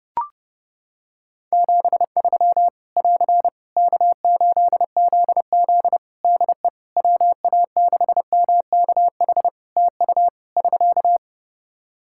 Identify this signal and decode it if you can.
Morse code, heard as a keyed tone, about 30 wpm, 700 Hz. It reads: 73 <AR> K8ZZ DE WA6MKH TU <SK>
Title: K1USN SST QSO Practice 30wpm